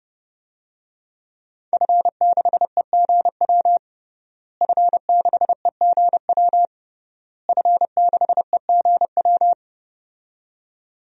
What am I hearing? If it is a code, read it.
Morse code, heard as a keyed tone, about 32 wpm, 700 Hz. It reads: F6EGW F6EGW F6EGW
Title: DXCC-100 Call Signs - 5 Characters - Repeated 3x - 30wpm